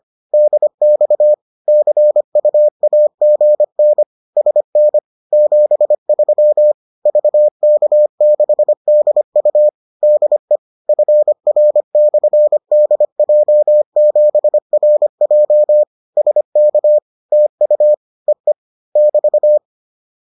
Morse code, heard as a keyed tone, about 25 wpm, 600 Hz. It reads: DX CUAGN SN 73 VK6DU DE FR/DJ7RJ SK TU EE <BT>